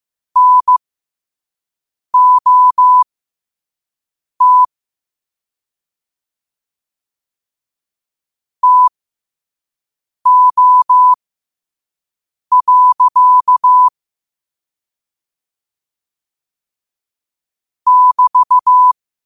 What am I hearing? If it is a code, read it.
Morse code, heard as a keyed tone, about 15 wpm, spaced as for 5 wpm, 1000 Hz. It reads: NOT TO. <BT>
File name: qotdfiles / 5wpm